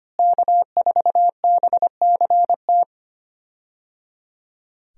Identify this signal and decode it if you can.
Morse code, heard as a keyed tone, about 25 wpm, 700 Hz. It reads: K4BCT
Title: USA Call Signs - 5 Characters - Repeated 3x - 25wpm